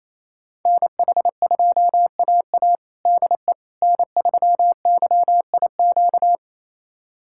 Morse code, heard as a keyed tone, about 28 wpm, 700 Hz. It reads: NH2AA DE N3YIQ
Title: USA Callsign DE Callsign - 5 Characters - 28wpm